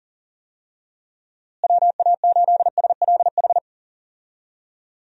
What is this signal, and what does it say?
Morse code, heard as a keyed tone, about 40 wpm, 700 Hz. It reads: WA8SLH
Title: USA Call Signs - 6 Characters - Repeated 3x - 40wpm